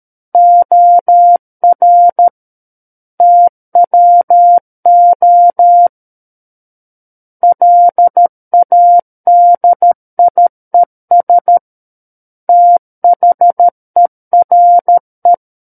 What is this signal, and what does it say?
Morse code, heard as a keyed tone, about 13 wpm, 700 Hz. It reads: OR TWO LADIES THERE
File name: Book One / 04-Chapter 4 - 13 WPM